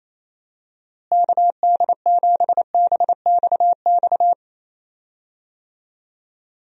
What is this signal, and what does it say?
Morse code, heard as a keyed tone, about 28 wpm, 700 Hz. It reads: KD7BXX
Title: USA Call Signs - 6 Characters - Repeated 3x - 28wpm